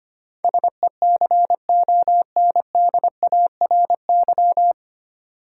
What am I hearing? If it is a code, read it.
Morse code, heard as a keyed tone, about 25 wpm, 700 Hz. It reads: SECONDARY
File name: Nine Letter Words - 25wpm